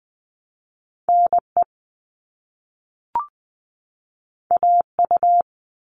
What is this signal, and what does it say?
Morse code, heard as a keyed tone, about 20 wpm, 700 Hz. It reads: NE AU
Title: All Cut Numbers - 2-Digit 20wpm